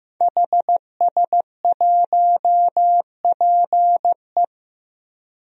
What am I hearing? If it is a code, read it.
Morse code, heard as a keyed tone, about 15 wpm, 700 Hz. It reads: HS1PE
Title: DXCC-100 Call Signs - 5 Characters - 15wpm